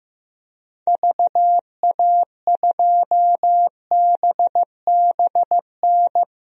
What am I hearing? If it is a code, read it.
Morse code, heard as a keyed tone, about 15 wpm, 700 Hz. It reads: VA2BBN
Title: DXCC-100 Call Signs - 6 Characters - 15wpm